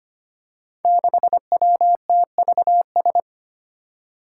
Morse code, heard as a keyed tone, about 25 wpm, 700 Hz. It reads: 6WTVS
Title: Five Challenging Letter-Number Combinations 25wpm